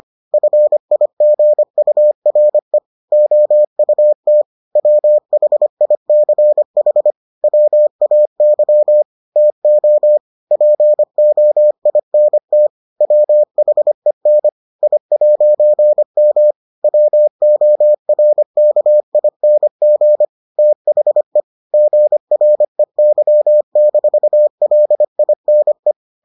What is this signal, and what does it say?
Morse code, heard as a keyed tone, about 25 wpm, 600 Hz. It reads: FIGURE OUT WHICH WAY TO POINT WHEN I'M WORKING THE GREY-LINE